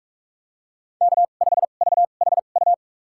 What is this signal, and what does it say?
Morse code, heard as a keyed tone, about 47 wpm, spaced as for 33 wpm, 700 Hz. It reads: K5VHU
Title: USA Call Signs - 2x Character Spacing - 45wpm